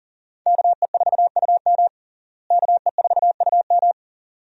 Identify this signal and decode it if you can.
Morse code, heard as a keyed tone, about 40 wpm, 700 Hz. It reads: KE4UM KE4UM
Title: USA Call Signs - 5 Characters - Repeated 2x - 40wpm